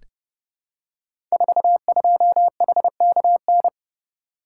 Morse code, heard as a keyed tone, about 30 wpm, 700 Hz. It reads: 42HKN
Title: Code Groups - 5 Letter:Numbers - 30wpm